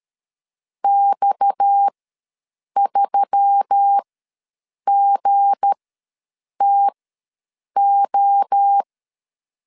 Morse code, heard as a keyed tone, about 13 wpm, spaced as for 7 wpm, 800 Hz. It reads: X3GTO
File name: LEZIONI / 7 GF9Z / 7 G ws 12wpm 3140